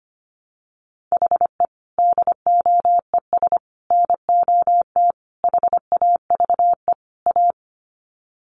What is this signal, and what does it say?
Morse code, heard as a keyed tone, about 25 wpm, 700 Hz. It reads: HE DOES NOT HAVE A
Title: Sets of 5 Words Encore 25wpm